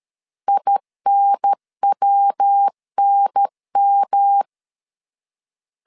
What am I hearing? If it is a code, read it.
Morse code, heard as a keyed tone, about 13 wpm, 800 Hz. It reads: INWNM
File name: CW / DET / 1 3 1500C S B 12wpm 2828